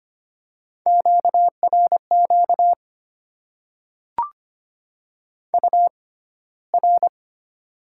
Morse code, heard as a keyed tone, about 25 wpm, spaced as for 15 wpm, 690 Hz. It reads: QRQ UR
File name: Top 90 QSO Elements - 6x Character Spacing - 25wpm